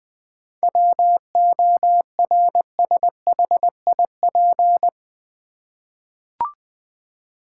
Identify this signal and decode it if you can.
Morse code, heard as a keyed tone, about 20 wpm, 700 Hz. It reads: WORSHIP
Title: 7 Letter Words - First 3 Letters Given - 20wpm